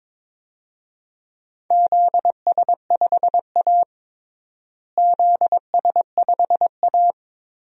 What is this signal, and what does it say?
Morse code, heard as a keyed tone, about 22 wpm, 700 Hz. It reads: ZS5A ZS5A
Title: DXCC-100 Call Signs - 4 Characters - Repeated 2x - 22wpm